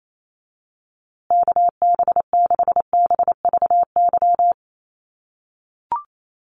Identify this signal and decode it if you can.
Morse code, heard as a keyed tone, about 28 wpm, 700 Hz. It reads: KB6BVY
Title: USA Call Signs Repeated 2x 28wpm